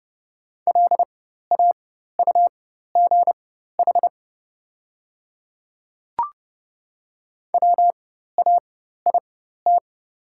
Morse code, heard as a keyed tone, about 30 wpm, spaced as for 14 wpm, 690 Hz. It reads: LAUGH WAIT
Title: 4x Character Spacing - Top 500 Words - 30wpm